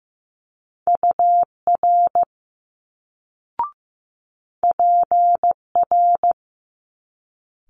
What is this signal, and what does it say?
Morse code, heard as a keyed tone, about 15 wpm, 700 Hz. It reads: UR PR
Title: Multi-letter Phonemes Quick Fire 15wpm